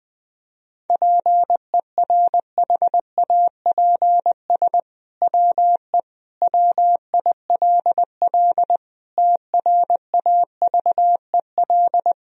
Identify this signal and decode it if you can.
Morse code, heard as a keyed tone, about 20 wpm, 700 Hz. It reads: PERHAPS WE WILL TRAVEL
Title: Sentences from Top 1000 Words - Review Part 2 - 20wpm